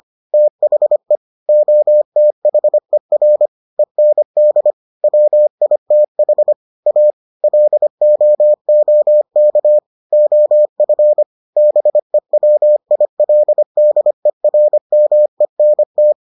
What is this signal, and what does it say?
Morse code, heard as a keyed tone, about 25 wpm, 600 Hz. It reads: THE OTHER END WITH A LOOK OF BEWILDERMENT